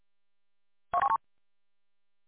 There is no code.